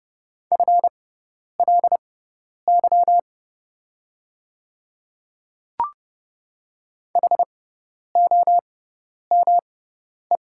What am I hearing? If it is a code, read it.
Morse code, heard as a keyed tone, about 30 wpm, spaced as for 10 wpm, 700 Hz. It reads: FLY HOME
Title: 6x Character Spacing - Top 500 Words - 30wpm